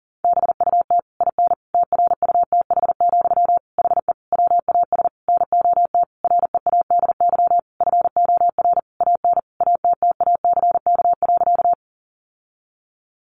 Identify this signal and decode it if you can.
Morse code, heard as a keyed tone, about 40 wpm, 700 Hz. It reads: BUT IN TRUTH, HE WAS NOT READY FOR AN ATTACK.
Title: Sentences from Top 1500 Words 40wpm